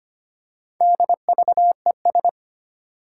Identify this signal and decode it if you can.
Morse code, heard as a keyed tone, about 25 wpm, 700 Hz. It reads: DVES
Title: Four Challenging Letter-Number Combinations 25wpm